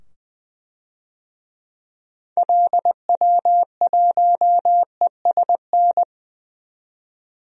Morse code, heard as a keyed tone, about 20 wpm, 700 Hz. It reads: LW1ESN
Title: DXCC-100 Call Signs - 6 Characters - 20wpm